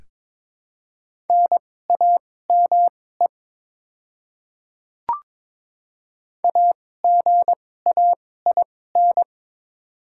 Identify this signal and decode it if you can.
Morse code, heard as a keyed tone, about 22 wpm, spaced as for 16 wpm, 700 Hz. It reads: NAME AGAIN
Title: 2x Character Spacing - Top 200 Words - 22wpm